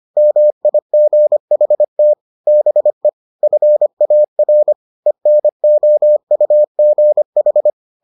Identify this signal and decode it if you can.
Morse code, heard as a keyed tone, about 25 wpm, 600 Hz. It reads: MIGHT BE FAR ENOUGH